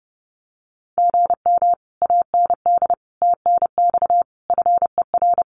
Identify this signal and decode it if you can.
Morse code, heard as a keyed tone, about 30 wpm, 700 Hz. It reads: GM AND TNX FER
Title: Common QSO Phrases Encore 30wpm